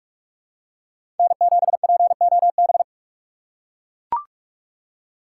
Morse code, heard as a keyed tone, about 43 wpm, 700 Hz. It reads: N7POB
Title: USA Call Signs - 5 Characters - 6x Character Spacing - 45wpm